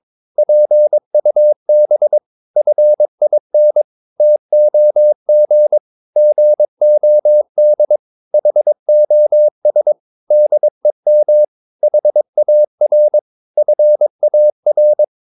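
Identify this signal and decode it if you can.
Morse code, heard as a keyed tone, about 22 wpm, 600 Hz. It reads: PUB FIN TOG GOD HOS DEM HAR FAR
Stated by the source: Morse practice of short Danish words at 22 wpm, from CW med Gnister.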